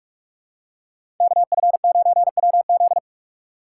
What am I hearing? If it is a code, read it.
Morse code, heard as a keyed tone, about 45 wpm, 700 Hz. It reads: KF9WZ
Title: USA Call Signs - 5 Characters - 3x Character Spacing - 45wpm